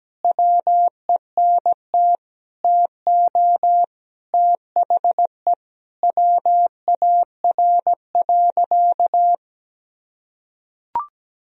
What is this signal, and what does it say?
Morse code, heard as a keyed tone, about 17 wpm, 700 Hz. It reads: WENT TO THE WAR.
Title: Sentences from Top 400 Words and Abbreviations 17wpm